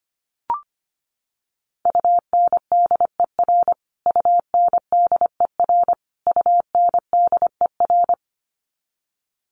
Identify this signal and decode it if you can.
Morse code, heard as a keyed tone, about 25 wpm, 700 Hz. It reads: UNDER UNDER UNDER
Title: Top 500 Words Repeated 25wpm